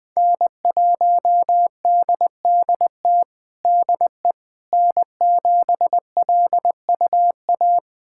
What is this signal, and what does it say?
Morse code, heard as a keyed tone, about 21 wpm, 700 Hz. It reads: N1DDT DE N7LUA
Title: USA Callsign DE Callsign - 5 Characters - 20wpm